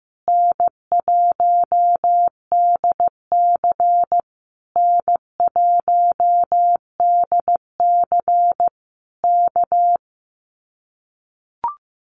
Morse code, heard as a keyed tone, about 15 wpm, 700 Hz. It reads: N1DC N1DC K
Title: Common QSO Phrases Encore 15wpm